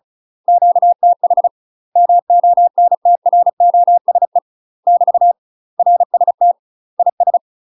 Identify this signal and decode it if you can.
Morse code, heard as a keyed tone, about 36 wpm, 700 Hz. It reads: QTH MONTROSE <BT> RST IS
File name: UK MASTER QSOs TRAINING / 35WPM / UK MASTER QSOs 0029